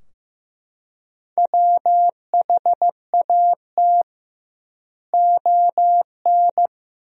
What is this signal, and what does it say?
Morse code, heard as a keyed tone, about 15 wpm, 700 Hz. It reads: WHAT ON
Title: 2x Word-Spacing - Sentences from Top 200 Words - 15wpm